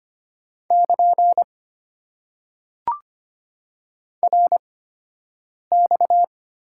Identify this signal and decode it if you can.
Morse code, heard as a keyed tone, about 25 wpm, spaced as for 7 wpm, 700 Hz. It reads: <KN> RX
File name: Top 90 QSO Elements - 8x Character Spacing - 25wpm